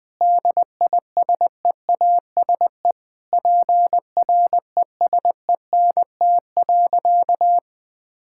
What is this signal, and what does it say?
Morse code, heard as a keyed tone, about 20 wpm, 700 Hz. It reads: DISEASE PRESENT.
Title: Sentences from Top 1000 Words - Review Part 6 - 20wpm